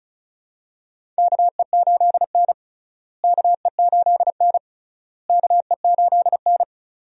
Morse code, heard as a keyed tone, about 36 wpm, 700 Hz. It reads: KE8N KE8N KE8N
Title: USA Call Signs - 4 Characters - Repeated 3x - 35wpm